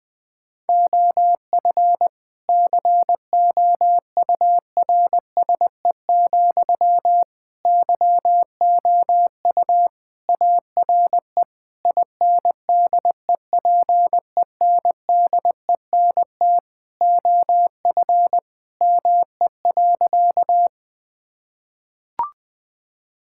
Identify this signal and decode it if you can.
Morse code, heard as a keyed tone, about 20 wpm, 700 Hz. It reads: OF COURSE, YOU ARE INDEPENDENT OF ME.
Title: Sentences from Top 1000 Words - Review Part 6 - 20wpm